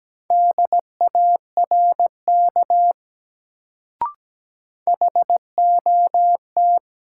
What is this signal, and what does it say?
Morse code, heard as a keyed tone, about 17 wpm, 700 Hz. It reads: DARK HOT
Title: Sentences from Top 500 Words - New Word Drill - 17wpm